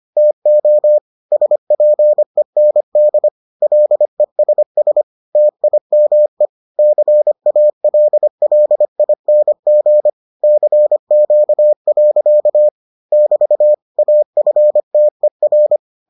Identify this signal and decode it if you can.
Morse code, heard as a keyed tone, about 25 wpm, 600 Hz. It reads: TO SPEND LESS TIME CALLING CQ. <BT> AFTER